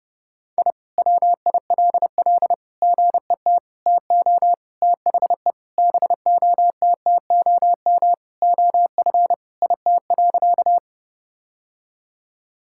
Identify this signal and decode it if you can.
Morse code, heard as a keyed tone, about 30 wpm, 700 Hz. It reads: I WILL GET TO THE BOTTOM OF IT.
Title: Sentences from Top 1000 Words - Review Part 6 - 30wpm